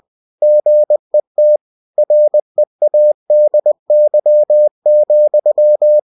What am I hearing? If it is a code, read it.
Morse code, heard as a keyed tone, about 20 wpm, 600 Hz. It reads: GET READY,
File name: cw-books / winnie-the-pooh / winnie-20wpm-page-0077